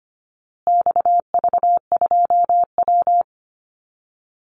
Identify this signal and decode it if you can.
Morse code, heard as a keyed tone, about 25 wpm, 700 Hz. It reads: XV2W
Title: Call Signs Repeated 3x - 25wpm